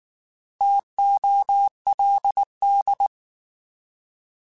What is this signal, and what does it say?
Morse code, heard as a keyed tone, about 19 wpm, 800 Hz. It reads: TOLD